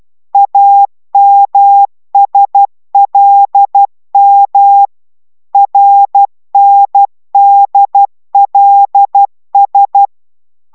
Morse code, heard as a keyed tone, about 12 wpm, 800 Hz. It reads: AMSLM RNDLS